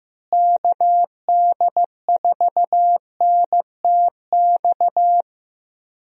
Morse code, heard as a keyed tone, about 15 wpm, 700 Hz. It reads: KD4NTX
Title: USA Call Signs - 6 Characters - Repeated 2x - 15wpm